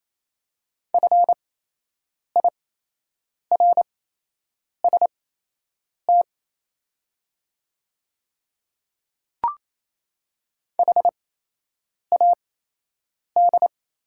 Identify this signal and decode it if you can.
Morse code, heard as a keyed tone, about 28 wpm, spaced as for 8 wpm, 700 Hz. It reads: FIRST HAD
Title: 8x Character Spacing - Top 100 Words - 28wpm